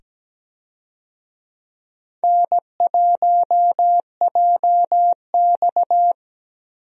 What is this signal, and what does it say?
Morse code, heard as a keyed tone, about 17 wpm, 700 Hz. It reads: N1JX
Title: USA Call Signs - 4 Characters - 17wpm